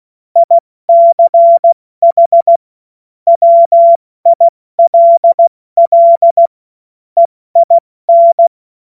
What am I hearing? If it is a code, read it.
Morse code, heard as a keyed tone, about 16 wpm, 680 Hz. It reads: ICH WILL EIN
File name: Witz-08-Juli-2025-70-F80